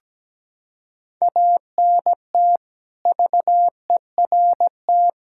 Morse code, heard as a keyed tone, about 17 wpm, 700 Hz. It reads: ANT VERT
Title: K1USN SST QSO Practice 17wpm